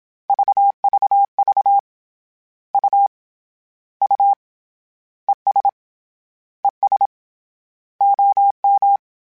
Morse code, heard as a keyed tone, about 27 wpm, 800 Hz. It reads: VVV U U ES ES OM
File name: Abreviations / Abreviations mp3s / Abreviations@25